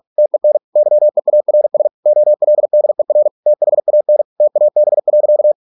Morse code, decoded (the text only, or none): TEN YEARS OLDER THAN TAB.